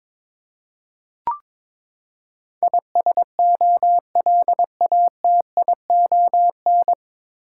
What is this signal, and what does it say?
Morse code, heard as a keyed tone, about 22 wpm, 700 Hz. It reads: ISOLATION
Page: Morse Code Ninja practice 22wpm